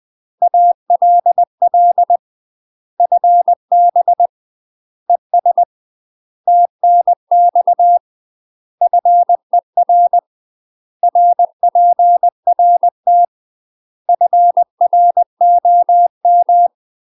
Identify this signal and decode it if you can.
Morse code, heard as a keyed tone, about 20 wpm, 700 Hz. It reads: ALL FB ES TNX FER RPRT FROM
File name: UK MASTER QSOs TRAINING / 20WPM / UK MASTER QSOs 0074